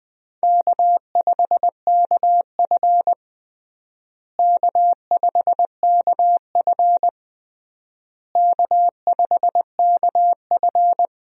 Morse code, heard as a keyed tone, about 20 wpm, 700 Hz. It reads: K5KF K5KF K5KF
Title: USA Call Signs - 4 Characters - Repeated 3x - 20wpm